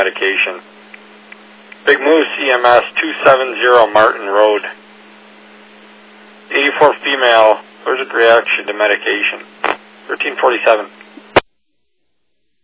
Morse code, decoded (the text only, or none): none